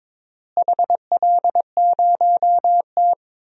Morse code, heard as a keyed tone, about 21 wpm, 700 Hz. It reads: HL0T
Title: DXCC-100 Call Signs - 4 Characters - Repeated 3x - 22wpm